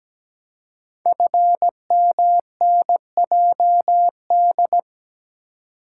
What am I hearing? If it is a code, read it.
Morse code, heard as a keyed tone, about 17 wpm, 700 Hz. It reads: FMNJD
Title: Code Groups - 5 Letters - 17wpm